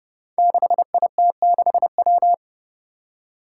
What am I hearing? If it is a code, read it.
Morse code, heard as a keyed tone, about 30 wpm, 700 Hz. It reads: 6IT6W
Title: Five Challenging Letter-Number Combinations 30wpm